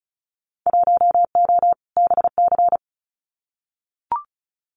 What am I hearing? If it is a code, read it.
Morse code, heard as a keyed tone, about 36 wpm, 700 Hz. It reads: 1O BC
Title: Winter Field Day Exchanges Encore 35wpm